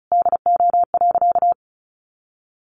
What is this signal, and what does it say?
Morse code, heard as a keyed tone, about 34 wpm, 700 Hz. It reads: DO.
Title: Sentences from Top 1400 Words 35wpm